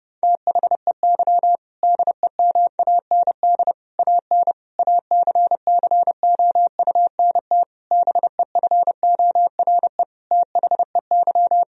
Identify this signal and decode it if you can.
Morse code, heard as a keyed tone, about 30 wpm, 700 Hz. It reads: THEY DEMAND AN ACCOUNT BEFORE THEY
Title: Sentences from Top 1000 Words - Review Part 4 - 30wpm